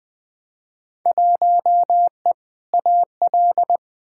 Morse code, heard as a keyed tone, about 20 wpm, 700 Hz. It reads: 1E AL
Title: Field Day Exchanges - 20wpm